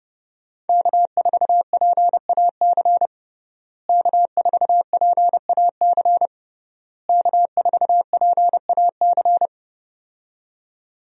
Morse code, heard as a keyed tone, about 30 wpm, 700 Hz. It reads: K4PAC K4PAC K4PAC
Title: USA Call Signs - 5 Characters - Repeated 3x - 30wpm